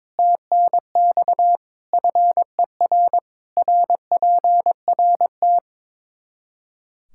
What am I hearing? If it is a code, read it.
Morse code, heard as a keyed tone, about 22 wpm, 700 Hz. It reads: TNX FER RPRT
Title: Practice QSO - Exchange 2 - Info - 22wpm